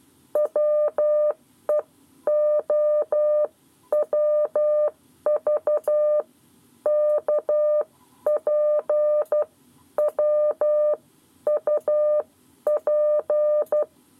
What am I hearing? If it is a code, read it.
Morse code, heard as a keyed tone, about 12 wpm, spaced as for 10 wpm, 600 Hz. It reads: WEOWV KP WUP
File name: 7 Lezione cw 1